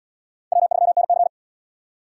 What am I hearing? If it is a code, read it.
Morse code, heard as a keyed tone, about 71 wpm, 700 Hz. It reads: W3NC